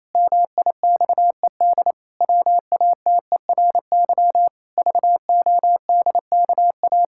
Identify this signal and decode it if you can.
Morse code, heard as a keyed tone, about 28 wpm, 700 Hz. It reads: MIXED WATERY VODKA
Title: Pangram Challenge v2 - 28wpm